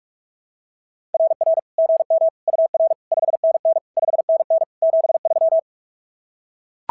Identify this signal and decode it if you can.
Morse code, heard as a keyed tone, about 44 wpm, 690 Hz. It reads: RR GM UR 5NN 5NN 73
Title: SOTA QSO Practice 45wpm